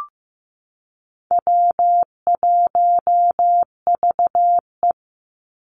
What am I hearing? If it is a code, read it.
Morse code, heard as a keyed tone, about 15 wpm, 700 Hz. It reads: W1VE
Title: Easy Call Signs Repeated 2x - 15wpm